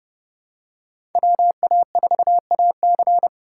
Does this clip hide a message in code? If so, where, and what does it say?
Yes: Morse code from 1.1 to 3.3 s, heard as a keyed tone, about 27 wpm, 700 Hz. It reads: WA4AC